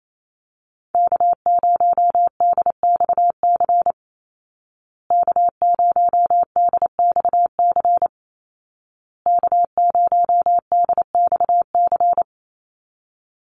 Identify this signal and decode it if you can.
Morse code, heard as a keyed tone, about 28 wpm, 700 Hz. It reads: K0DXC K0DXC K0DXC
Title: Call Signs Repeated 2x - 28wpm